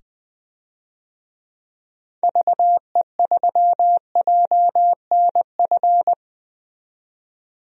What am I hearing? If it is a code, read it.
Morse code, heard as a keyed tone, about 20 wpm, 700 Hz. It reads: VE3JNF
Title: DXCC-100 Call Signs - 6 Characters - 20wpm